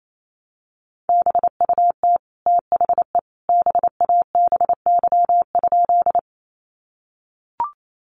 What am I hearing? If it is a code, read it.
Morse code, heard as a keyed tone, about 28 wpm, 700 Hz. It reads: BUT THE BABY?
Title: Sentences from Top 1300 Words 28wpm